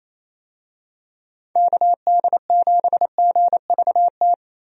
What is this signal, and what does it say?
Morse code, heard as a keyed tone, about 28 wpm, 700 Hz. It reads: KD7GVT
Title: USA Call Signs - 6 Characters - 28wpm